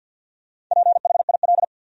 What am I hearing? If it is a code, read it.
Morse code, heard as a keyed tone, about 52 wpm, 700 Hz. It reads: PHIL
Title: USA Common Names - Max 4 Letters - 50wpm